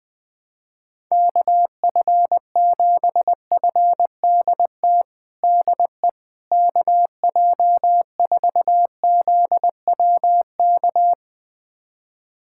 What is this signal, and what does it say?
Morse code, heard as a keyed tone, about 20 wpm, 700 Hz. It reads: KF7FDT DE KJ4ZWK
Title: USA Callsign DE Callsign - 6 Characters - 20wpm